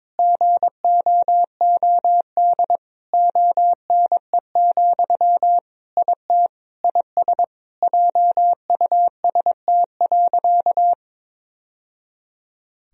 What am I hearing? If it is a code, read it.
Morse code, heard as a keyed tone, about 22 wpm, 700 Hz. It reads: GOOD ONE, IT IS JUST.
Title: Sentences from Top 1000 Words - Review Part 1 - 22wpm